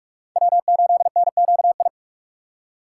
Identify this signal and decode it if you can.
Morse code, heard as a keyed tone, about 44 wpm, 700 Hz. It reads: W8NQI